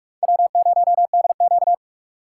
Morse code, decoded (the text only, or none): W0DQ